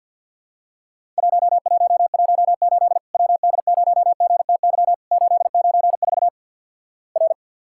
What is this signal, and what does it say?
Morse code, heard as a keyed tone, about 50 wpm, 700 Hz. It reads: 1118 WD0GTY 894 R
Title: Worked All Europe DX Contest - QTC Exchange with Repeats - 50wpm